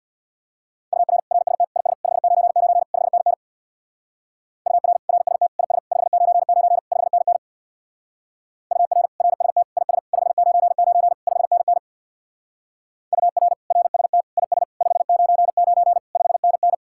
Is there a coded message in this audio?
Yes: Morse code from 0.9 to 16.7 s, heard as a keyed tone, about 64 wpm, 700 Hz. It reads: UR RST IS 599 5NN UR RST IS 599 5NN UR RST IS 599 5NN UR RST IS 599 5NN